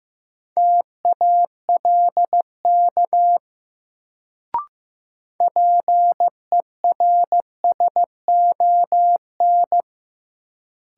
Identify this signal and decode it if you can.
Morse code, heard as a keyed tone, about 15 wpm, 700 Hz. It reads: TALK PERSON
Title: Sentences from Top 400 Words - New Word Drill - 15wpm